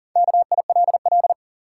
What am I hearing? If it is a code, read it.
Morse code, heard as a keyed tone, about 40 wpm, 700 Hz. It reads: KILL
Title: Sentences from Top 1400 Words - New Word Drill - 40wpm